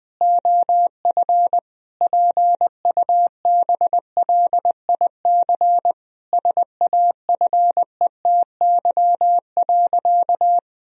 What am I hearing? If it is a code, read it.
Morse code, heard as a keyed tone, about 21 wpm, 700 Hz. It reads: OF PUBLIC SAFETY.
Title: Sentences from Top 1000 Words - Review Part 4 - 20wpm